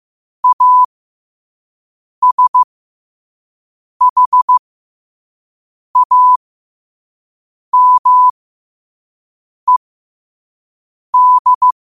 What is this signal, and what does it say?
Morse code, heard as a keyed tone, about 15 wpm, spaced as for 5 wpm, 1000 Hz. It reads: ASHAMED
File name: qotdfiles / 5wpm